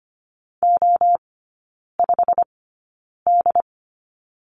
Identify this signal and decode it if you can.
Morse code, heard as a keyed tone, about 25 wpm, spaced as for 9 wpm, 700 Hz. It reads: O5D
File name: Lesson 17e - Letter u - Three Characters 25wpm